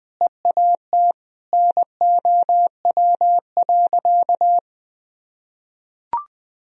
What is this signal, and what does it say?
Morse code, heard as a keyed tone, about 20 wpm, 700 Hz. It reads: EAT NOW.
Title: Sentences from Top 300 Words and Abbreviations 20wpm